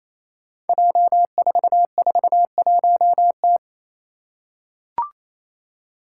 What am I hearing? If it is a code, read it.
Morse code, heard as a keyed tone, about 28 wpm, 700 Hz. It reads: J441T
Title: Five Challenging Letter-Number Combinations 28wpm